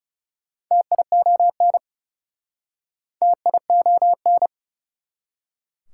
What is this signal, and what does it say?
Morse code, heard as a keyed tone, about 38 wpm, 690 Hz. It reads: TION TION